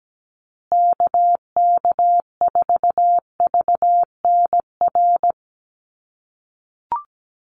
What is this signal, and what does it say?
Morse code, heard as a keyed tone, about 17 wpm, 700 Hz. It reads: KK4VNR
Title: USA Call Signs Repeated 2x 17wpm